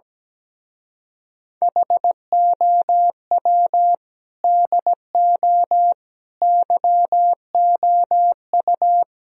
Morse code, heard as a keyed tone, about 17 wpm, 700 Hz. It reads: HOW DO YOU
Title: Speed-Racing - Sets of 4 Words 17wpm